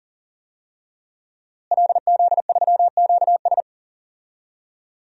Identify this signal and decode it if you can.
Morse code, heard as a keyed tone, about 40 wpm, 700 Hz. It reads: LZ3QS
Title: DXCC-100 Call Signs - 5 Characters - Repeated 3x - 40wpm